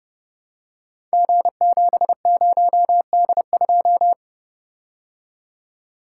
Morse code, heard as a keyed tone, about 30 wpm, 700 Hz. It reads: G70D2
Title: Code Groups - 5 Letter:Numbers - 30wpm